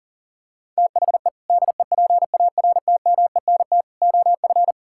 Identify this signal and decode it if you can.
Morse code, heard as a keyed tone, about 40 wpm, 700 Hz. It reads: THE DEPARTMENT OF
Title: Sentences from Top 1000 Words - Review Part 5 - 40wpm